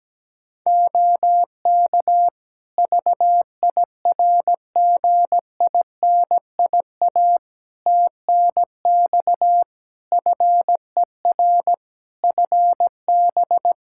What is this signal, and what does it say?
Morse code, heard as a keyed tone, about 17 wpm, 700 Hz. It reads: OK VIRGINIA TNX FER FB
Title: Practice QSO - Exchange 4 - Ending - 17wpm